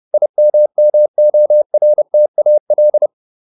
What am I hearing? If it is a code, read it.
Morse code, heard as a keyed tone, about 30 wpm, 600 Hz. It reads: IMMORTAL